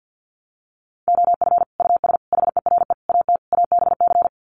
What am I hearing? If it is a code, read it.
Morse code, heard as a keyed tone, about 48 wpm, 700 Hz. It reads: OF US HERE AT ABC
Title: Sets of 5 Words 50wpm